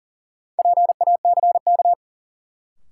Morse code, heard as a keyed tone, about 37 wpm, 700 Hz. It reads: PACK
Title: Sentences from Top 1300 Words - New Word Drill - 40wpm